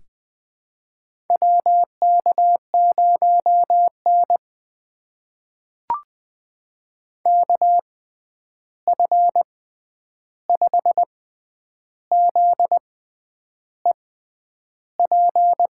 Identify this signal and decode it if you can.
Morse code, heard as a keyed tone, about 20 wpm, spaced as for 9 wpm, 700 Hz. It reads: WK0N KF5ZEP
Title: USA Call Signs - 6x Character Spacing - 20wpm